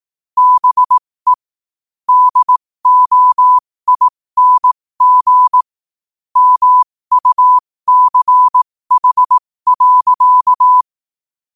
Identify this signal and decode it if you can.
Morse code, heard as a keyed tone, about 18 wpm, spaced as for 15 wpm, 1000 Hz. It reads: BE DOING MUCH.
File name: qotdfiles / 15wpm